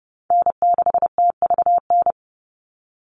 Morse code, heard as a keyed tone, about 30 wpm, 700 Hz. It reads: N6TVN
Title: Call Signs - 30wpm